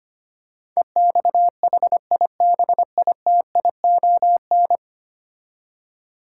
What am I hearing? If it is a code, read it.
Morse code, heard as a keyed tone, about 25 wpm, 700 Hz. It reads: EXHIBITION
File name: 10 Letter Words - First 3 Letters Given - 25wpm